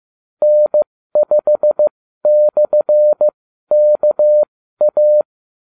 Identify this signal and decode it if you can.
Morse code, heard as a keyed tone, about 15 wpm, spaced as for 12 wpm, 600 Hz. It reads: N5/KA
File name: cw / ubacw34 3